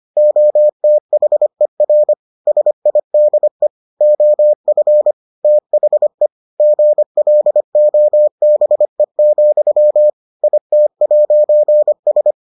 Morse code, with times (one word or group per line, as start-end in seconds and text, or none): 0.2-2.1 OTHER
2.5-3.7 SIDE
4.0-5.1 OF
5.4-6.3 THE
6.6-10.1 GLOBE,
10.4-12.3 IT'S